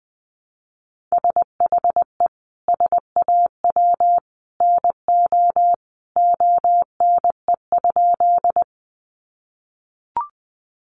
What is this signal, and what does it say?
Morse code, heard as a keyed tone, about 20 wpm, 700 Hz. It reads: SHE SAW NO ONE?
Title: Sentences from Top 300 Words 20wpm